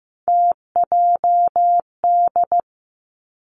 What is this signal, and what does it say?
Morse code, heard as a keyed tone, about 15 wpm, 700 Hz. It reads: TJD
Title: Three Challenging Letter-Number Combinations 15wpm